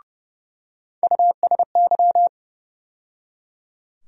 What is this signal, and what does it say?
Morse code, heard as a keyed tone, about 30 wpm, 700 Hz. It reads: USY
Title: USA Call Sign Postfixes - 3 Letters 30wpm